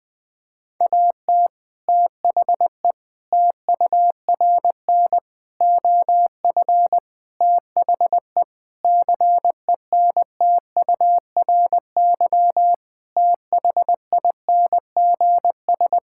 Morse code, heard as a keyed tone, about 20 wpm, 700 Hz. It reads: AT THE TURN OF THE CENTURY THINGS
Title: Sentences with 4-Word Phrases - 20wpm